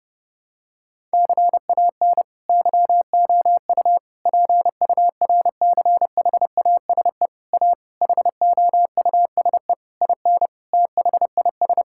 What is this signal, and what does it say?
Morse code, heard as a keyed tone, about 30 wpm, 700 Hz. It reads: CAN YOU PURCHASE A HOUSE IN THIS
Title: Sentences from Top 1000 Words - Review Part 5 - 30wpm